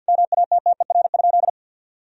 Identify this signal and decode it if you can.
Morse code, heard as a keyed tone, about 52 wpm, 700 Hz. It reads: MATTER?
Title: Sentences from Top 1000 Words - Review Part 1 - 50wpm